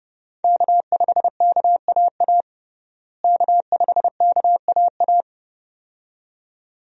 Morse code, heard as a keyed tone, about 32 wpm, 700 Hz. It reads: K5KAA K5KAA
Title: USA Call Signs - 5 Characters - Repeated 2x - 30wpm